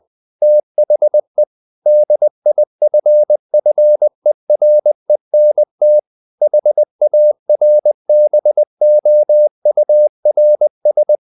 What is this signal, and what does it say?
Morse code, heard as a keyed tone, about 20 wpm, 600 Hz. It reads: THE DIFFERENT HARBOURS
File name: the old man and the sea / the old man and the sea part 1 20wpm